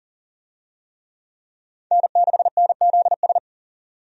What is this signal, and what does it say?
Morse code, heard as a keyed tone, about 40 wpm, 700 Hz. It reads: N6NZS
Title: USA Call Signs - 5 Characters - 40wpm